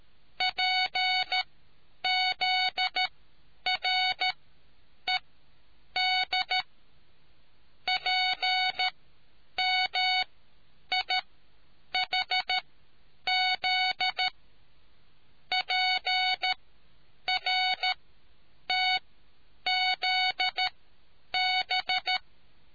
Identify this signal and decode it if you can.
Morse code, heard as a keyed tone, about 13 wpm, spaced as for 8 wpm, 740 Hz. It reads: PZRED PMIHZ PRTZB